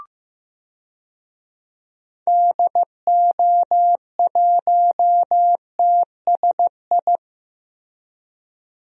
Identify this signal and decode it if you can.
Morse code, heard as a keyed tone, about 15 wpm, 700 Hz. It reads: DO1TSI
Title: DXCC-100 Call Signs - 6 Characters - Repeated 3x - 15wpm